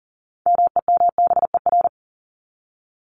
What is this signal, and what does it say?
Morse code, heard as a keyed tone, about 40 wpm, 700 Hz. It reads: MEMBER
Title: ES:S Suffix - Top 100 Words with Base Words 40wpm